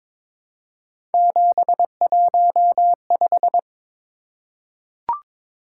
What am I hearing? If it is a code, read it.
Morse code, heard as a keyed tone, about 21 wpm, 700 Hz. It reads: 715
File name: practice / 22wpm / Three Letter-Number Combinations - 22wpm